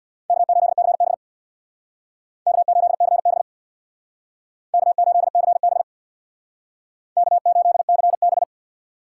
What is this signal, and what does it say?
Morse code, heard as a keyed tone, about 64 wpm, 700 Hz. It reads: K8CB K8CB K8CB K8CB